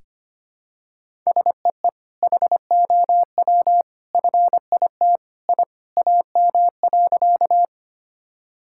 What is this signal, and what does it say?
Morse code, heard as a keyed tone, about 25 wpm, 700 Hz. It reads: SEE HOW FIT I AM.
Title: Sentences from Top 1000 Words - Review Part 5 - 25wpm